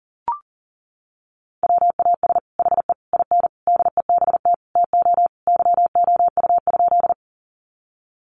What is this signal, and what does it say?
Morse code, heard as a keyed tone, about 40 wpm, 700 Hz. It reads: WAS HE IN DEBT TO YOU?
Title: Sentences from Top 1300 Words 40wpm